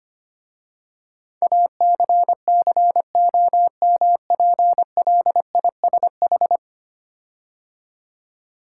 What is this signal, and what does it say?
Morse code, heard as a keyed tone, about 25 wpm, 700 Hz. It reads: ACCOMPLISH
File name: 10 Letter Words - First 3 Letters Given - 25wpm